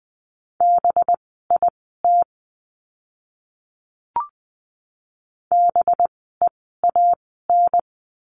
Morse code, heard as a keyed tone, about 20 wpm, spaced as for 15 wpm, 700 Hz. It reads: BIT BEAN
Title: 185 Lesson 30f - Letter b - Words 20wpm